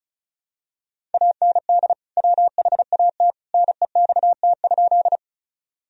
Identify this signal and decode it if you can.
Morse code, heard as a keyed tone, about 36 wpm, 700 Hz. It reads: AND WHAT NEXT?
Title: Sentences from Top 1000 Words - Review Part 1 - 35wpm